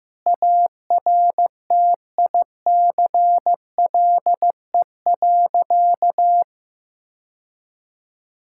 Morse code, heard as a keyed tone, about 15 wpm, 700 Hz. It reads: ARTICLE.
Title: Sentences from Top 1000 Words - Review Part 5 - 15wpm